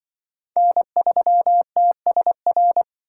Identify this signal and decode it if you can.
Morse code, heard as a keyed tone, about 25 wpm, 700 Hz. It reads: N3TSR
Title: Speed Racing - USA Call Signs 20wpm